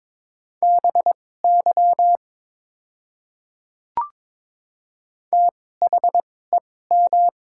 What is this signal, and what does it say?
Morse code, heard as a keyed tone, about 22 wpm, spaced as for 16 wpm, 700 Hz. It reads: BY THEM
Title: 2x Character Spacing - Top 100 Words - 22wpm